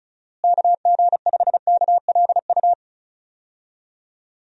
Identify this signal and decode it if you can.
Morse code, heard as a keyed tone, about 34 wpm, 700 Hz. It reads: KG5KLU
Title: USA Call Signs - 6 Characters - Repeated 2x - 35wpm